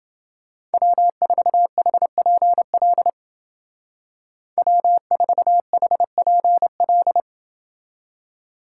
Morse code, heard as a keyed tone, about 28 wpm, 700 Hz. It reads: W4HPL W4HPL